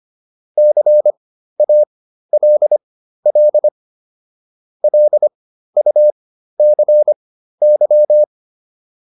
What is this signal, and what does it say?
Morse code, heard as a keyed tone, about 25 wpm, spaced as for 13 wpm, 600 Hz. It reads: CALL LUCY